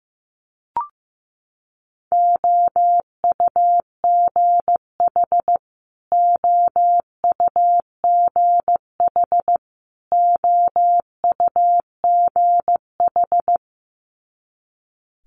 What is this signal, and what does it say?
Morse code, heard as a keyed tone, about 15 wpm, 700 Hz. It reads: OUGH OUGH OUGH
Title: Multi-letter Phonemes Repeated 3x 15wpm